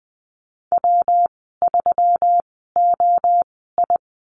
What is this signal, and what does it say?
Morse code, heard as a keyed tone, about 20 wpm, spaced as for 15 wpm, 700 Hz. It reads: W3OI
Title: 128 Lesson 21g - Number 6 - Call Signs 20wpm